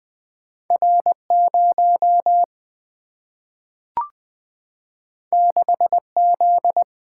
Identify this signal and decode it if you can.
Morse code, heard as a keyed tone, about 20 wpm, 700 Hz. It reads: R 0 6 Z
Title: Single-Letter Single-Number 20wpm